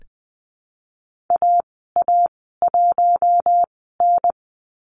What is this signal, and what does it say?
Morse code, heard as a keyed tone, about 20 wpm, spaced as for 15 wpm, 700 Hz. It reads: AA1N
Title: 042 Lesson 8g - Number 1 - Call Signs 20wpm